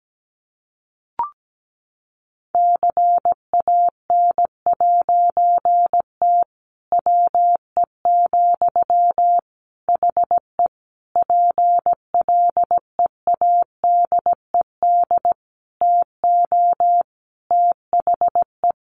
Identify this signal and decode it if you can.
Morse code, heard as a keyed tone, about 17 wpm, 700 Hz. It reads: CAN'T WE, HE PLEADED TO THE
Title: A Sound Of Thunder 17wpm